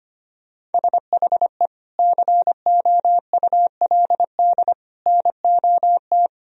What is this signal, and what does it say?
Morse code, heard as a keyed tone, about 25 wpm, 700 Hz. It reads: SHE COULD NOT
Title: Sentences from Top 1000 Words - Review Part 2 - 25wpm